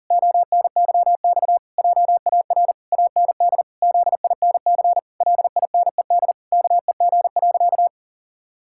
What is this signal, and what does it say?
Morse code, heard as a keyed tone, about 37 wpm, 700 Hz. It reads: ONYX JAR AND ZINC LINED KEG.
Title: Pangram Challenge v2 - 40wpm